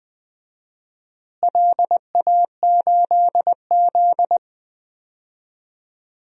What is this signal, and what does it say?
Morse code, heard as a keyed tone, about 20 wpm, 700 Hz. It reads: LA8Z
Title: DXCC-100 Call Signs - 4 Characters - Repeated 2x - 20wpm